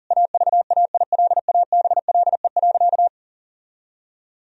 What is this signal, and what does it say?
Morse code, heard as a keyed tone, about 40 wpm, 700 Hz. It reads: AVAILABLE.
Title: Sentences from Top 1000 Words - Review Part 3 - 40wpm